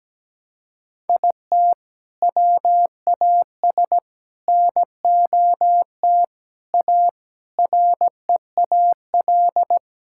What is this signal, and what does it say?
Morse code, heard as a keyed tone, about 17 wpm, 700 Hz. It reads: IT WAS NOT A REAL
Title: Sentences from Top 1000 Words - Review Part 6 - 17wpm